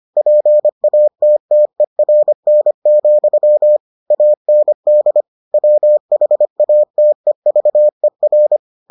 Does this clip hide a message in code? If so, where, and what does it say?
Yes: Morse code from 0.2 to 8.6 s, heard as a keyed tone, about 25 wpm, 600 Hz. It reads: PATTERN, AND WHATEVER